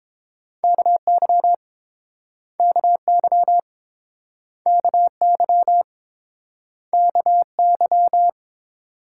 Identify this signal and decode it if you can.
Morse code, heard as a keyed tone, about 30 wpm, 700 Hz. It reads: KY KY KY KY